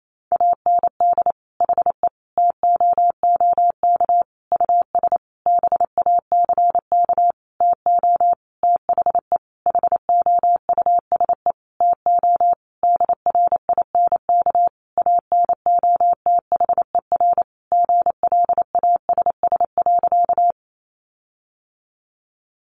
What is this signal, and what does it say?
Morse code, heard as a keyed tone, about 28 wpm, 700 Hz. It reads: AND HE TOOK US BACK TO THE HOUSE TO DRINK ANOTHER GLASS.
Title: Sentences from Top 1500 Words 28wpm